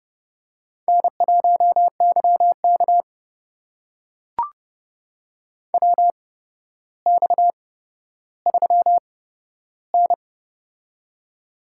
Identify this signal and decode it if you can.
Morse code, heard as a keyed tone, about 30 wpm, spaced as for 13 wpm, 700 Hz. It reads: N1YK WX3N
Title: USA Call Signs - 4 Characters - 8x Character Spacing - 30wpm